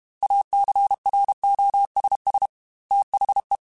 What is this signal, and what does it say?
Morse code, heard as a keyed tone, about 32 wpm, 800 Hz. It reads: ACROSS THE
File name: Technical Subjects / Technical Subjects-mp3s / XC @30